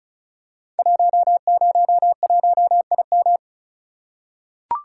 Morse code, heard as a keyed tone, about 36 wpm, 700 Hz. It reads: 101IM
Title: Code Groups - 5 Letter:Numbers - 35wpm